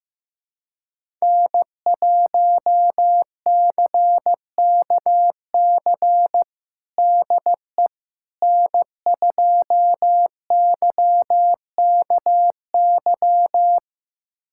Morse code, heard as a keyed tone, about 15 wpm, 700 Hz. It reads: N1CKC DE N2YKY
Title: USA Callsign DE Callsign - 5 Characters - 15wpm